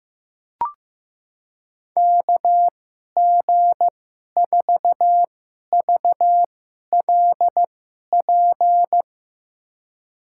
Morse code, heard as a keyed tone, about 15 wpm, spaced as for 11 wpm, 700 Hz. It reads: KG4VLP
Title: USA Call Signs - 6 Characters - 2x Character Spacing - 15wpm